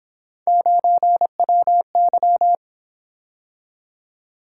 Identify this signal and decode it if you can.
Morse code, heard as a keyed tone, about 29 wpm, 700 Hz. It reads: 9WY